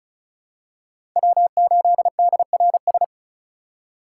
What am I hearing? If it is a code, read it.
Morse code, heard as a keyed tone, about 34 wpm, 700 Hz. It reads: W8DRS
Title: USA Call Signs - 5 Characters - 3x Character Spacing - 35wpm